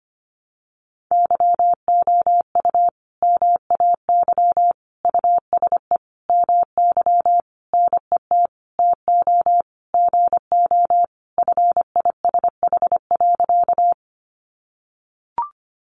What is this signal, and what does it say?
Morse code, heard as a keyed tone, about 25 wpm, 700 Hz. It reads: YOU MAY USE MY NET TO GO FISH.
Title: Sentences from Top 1400 Words 25wpm